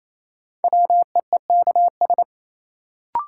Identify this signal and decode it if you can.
Morse code, heard as a keyed tone, about 28 wpm, 700 Hz. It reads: WEEKS
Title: Sets of 5 Words - New Word Drill - 28wpm